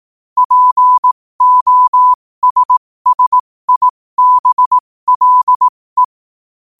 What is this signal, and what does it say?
Morse code, heard as a keyed tone, about 18 wpm, spaced as for 15 wpm, 1000 Hz. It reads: POSSIBLE
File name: qotdfiles / 15wpm